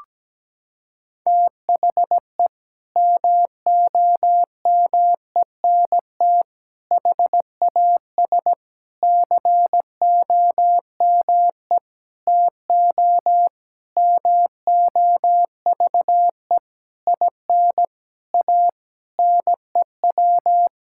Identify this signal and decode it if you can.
Morse code, heard as a keyed tone, about 17 wpm, 700 Hz. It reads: THE MOMENT HAS COME TO MOVE IN A NEW
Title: Sentences from Top 1000 Words - Review Part 5 - 17wpm